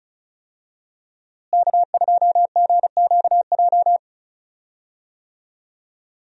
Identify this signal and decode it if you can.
Morse code, heard as a keyed tone, about 34 wpm, 700 Hz. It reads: K2GQJ
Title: USA Call Signs - 5 Characters - Repeated 2x - 35wpm